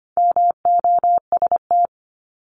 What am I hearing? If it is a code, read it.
Morse code, heard as a keyed tone, about 25 wpm, 700 Hz. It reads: MOST
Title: Top 200 Words 25wpm